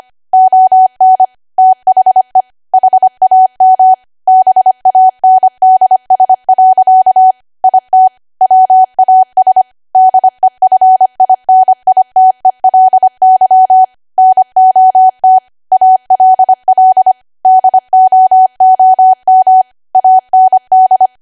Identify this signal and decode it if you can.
Morse code, heard as a keyed tone, about 25 wpm, 750 Hz. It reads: ON THE HAM BANDS. IT WAS DEFINITELY NOT ALL DOOM AND